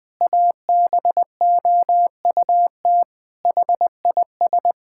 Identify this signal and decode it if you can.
Morse code, heard as a keyed tone, about 21 wpm, 700 Hz. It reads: ABOUT HIS
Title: Pangram Challenge v2 - 20wpm